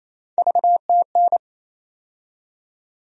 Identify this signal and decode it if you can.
Morse code, heard as a keyed tone, about 28 wpm, 700 Hz. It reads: VTN
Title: USA Call Sign Postfixes - 3 Letters 28wpm